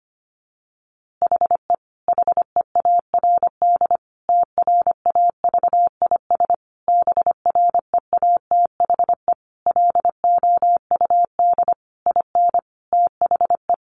Morse code, heard as a keyed tone, about 25 wpm, 700 Hz. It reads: HE HEARD TRAVIS BREATHE LOUD IN THE
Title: A Sound Of Thunder 25wpm